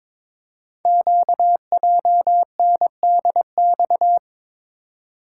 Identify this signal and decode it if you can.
Morse code, heard as a keyed tone, about 22 wpm, 700 Hz. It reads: QJNDX
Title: Code Groups - 5 Letters - 22wpm